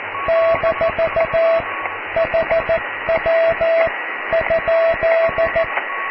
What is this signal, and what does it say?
Morse code, heard as a keyed tone, about 14 wpm, spaced as for 12 wpm, 660 Hz, with radio noise. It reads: - HW?
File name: cw / QSO+QRM